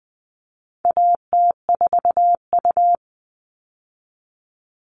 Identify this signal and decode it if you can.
Morse code, heard as a keyed tone, about 20 wpm, 700 Hz. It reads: AT4U
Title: All Cut Numbers - 4-Digit 20wpm